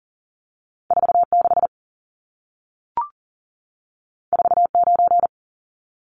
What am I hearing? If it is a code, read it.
Morse code, heard as a keyed tone, about 40 wpm, 700 Hz. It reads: 46 49
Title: Numbers - 2-Digit 40wpm